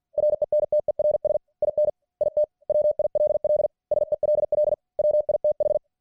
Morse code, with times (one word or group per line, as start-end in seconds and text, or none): 0.2-1.4 RENTERS
1.6-1.9 IN
2.2-2.4 IT
2.7-3.7 WILL
3.9-4.7 SELL
5.0-5.8 WITH